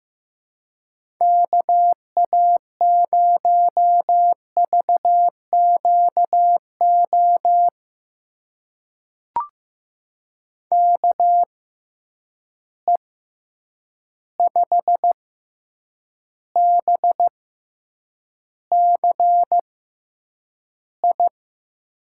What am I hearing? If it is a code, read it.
Morse code, heard as a keyed tone, about 15 wpm, spaced as for 8 wpm, 700 Hz. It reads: KA0VQO KE5BCI
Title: USA Call Signs - 6 Characters - 6x Character Spacing - 15wpm